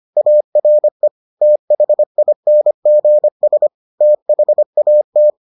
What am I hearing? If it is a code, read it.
Morse code, heard as a keyed tone, about 25 wpm, 600 Hz. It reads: ARE THINGS THAT